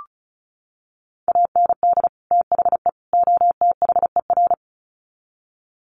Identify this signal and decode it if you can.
Morse code, heard as a keyed tone, about 34 wpm, 700 Hz. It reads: AND THE OTHER
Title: Sets of 3 Words 35wpm